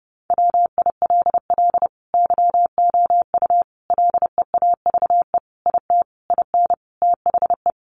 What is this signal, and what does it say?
Morse code, heard as a keyed tone, about 30 wpm, 700 Hz. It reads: WILL YOU LEAVE IT IN THE
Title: Sentences from Top 1300 Words 30wpm